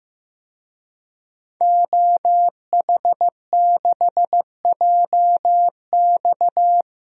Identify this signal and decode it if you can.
Morse code, heard as a keyed tone, about 15 wpm, 700 Hz. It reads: OH6JX
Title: DXCC-100 Call Signs - 5 Characters - 15wpm